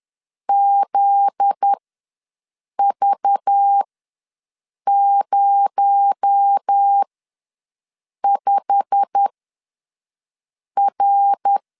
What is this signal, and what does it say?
Morse code, heard as a keyed tone, about 10 wpm, spaced as for 5 wpm, 800 Hz. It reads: ZV05R
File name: LEZIONI / 9 28 ws 10wpm 4124